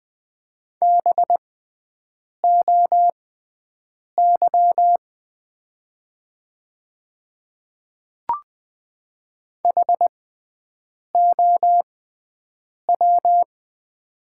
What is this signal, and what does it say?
Morse code, heard as a keyed tone, about 20 wpm, spaced as for 7 wpm, 700 Hz. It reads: BOY HOW
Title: 6x Character Spacing - Top 200 Words - 20wpm